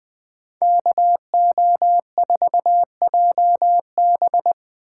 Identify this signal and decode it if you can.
Morse code, heard as a keyed tone, about 23 wpm, 700 Hz. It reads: KO4JB
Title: Speed Racing - USA 2x2 Call Signs 20wpm